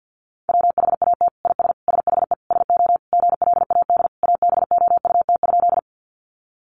Morse code, heard as a keyed tone, about 50 wpm, 700 Hz. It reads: WHAT IS SHE SO GLAD ABOUT?
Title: Sentences from Top 800 Words 50wpm